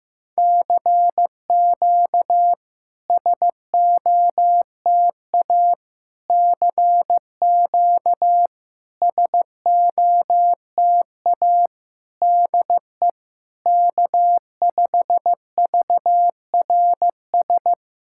Morse code, heard as a keyed tone, about 15 wpm, 700 Hz. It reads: CQ SOTA CQ SOTA DE K5VRS
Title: SOTA QSO Practice 15wpm